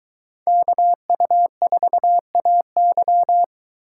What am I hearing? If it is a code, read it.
Morse code, heard as a keyed tone, about 22 wpm, 700 Hz. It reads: KU4AY